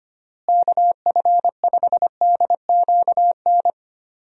Morse code, heard as a keyed tone, about 25 wpm, 700 Hz. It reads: KF5DQN
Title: USA Call Signs - 6 Characters - Repeated 2x - 25wpm